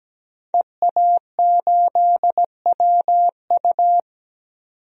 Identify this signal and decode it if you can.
Morse code, heard as a keyed tone, about 17 wpm, 700 Hz. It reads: EA8WU
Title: DXCC-100 Call Signs - 5 Characters - 17wpm